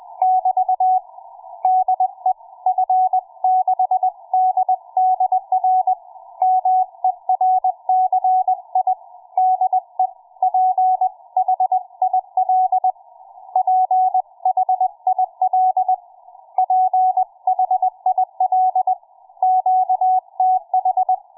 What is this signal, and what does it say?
Morse code, heard as a keyed tone, about 20 wpm, 740 Hz. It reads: <BT> DE F6DDR MERCI DE PHIL PHIL PHIL QTH
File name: cw forum / fm5cd 1833 le17avril08 06h51